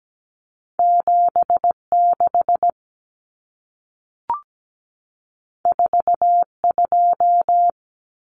Morse code, heard as a keyed tone, about 17 wpm, 700 Hz. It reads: 76 42
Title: Numbers - 2-Digit 17wpm